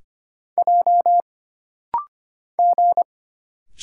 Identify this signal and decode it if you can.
Morse code, heard as a keyed tone, about 25 wpm, 700 Hz. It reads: J G
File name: Single Character (Letter:Number:Punctuation:Prosign) - Rapid-Fire - 25wpm